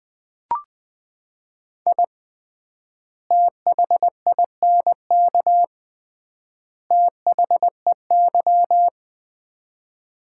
Morse code, heard as a keyed tone, about 20 wpm, 700 Hz. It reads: I THINK THEY
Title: 3x Word-Spacing - Sets of 3 Words - 20wpm